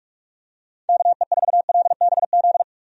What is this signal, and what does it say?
Morse code, heard as a keyed tone, about 41 wpm, 700 Hz. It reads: KE4LBZ
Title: USA Call Signs - 6 Characters - 45wpm